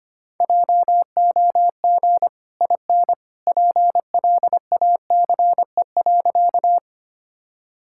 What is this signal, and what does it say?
Morse code, heard as a keyed tone, about 25 wpm, 700 Hz. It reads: JOG IN PLACE.
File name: Easy Sentences with Hints - 25wpm